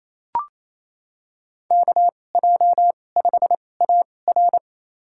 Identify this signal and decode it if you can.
Morse code, heard as a keyed tone, about 28 wpm, spaced as for 20 wpm, 700 Hz. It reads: KJ5AR
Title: USA Call Signs - 5 Characters - 2x Character Spacing - 28wpm